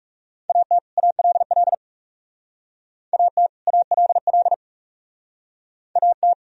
Morse code, heard as a keyed tone, about 36 wpm, 700 Hz. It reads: AT ALL AT ALL AT
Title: Speed-Racing - Sets of 2 Words 30wpm